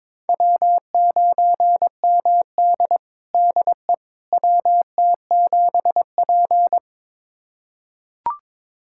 Morse code, heard as a keyed tone, about 22 wpm, 700 Hz. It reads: W9MD DE WT7P
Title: USA Callsign DE Callsign - 4 Characters - 22wpm